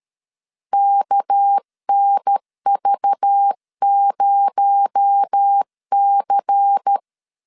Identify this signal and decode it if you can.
Morse code, heard as a keyed tone, about 13 wpm, 800 Hz. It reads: KNV0C
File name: LEZIONI / 7 GF9Z / 7 GF 12wpm 2122